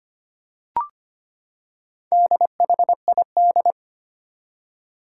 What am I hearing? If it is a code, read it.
Morse code, heard as a keyed tone, about 25 wpm, 700 Hz. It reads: DHID
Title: Four Challenging Letter-Number Combinations 25wpm